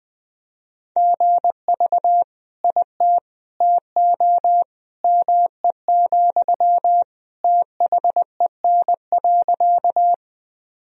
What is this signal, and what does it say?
Morse code, heard as a keyed tone, about 20 wpm, 700 Hz. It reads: GV IT TO ME, THEN.
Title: Sentences from Top 200 Words and Abbreviations 20wpm